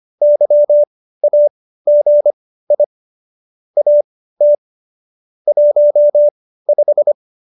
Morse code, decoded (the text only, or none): YAGI AT 15